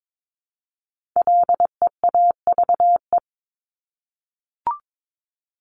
Morse code, heard as a keyed tone, about 22 wpm, 700 Hz. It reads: LEAVE
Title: Top 500 Words Repeated 22wpm